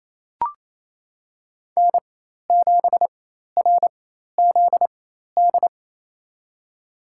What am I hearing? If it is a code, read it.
Morse code, heard as a keyed tone, about 28 wpm, spaced as for 13 wpm, 700 Hz. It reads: N7RZD
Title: USA Call Signs - 5 Characters - 4x Character Spacing - 28wpm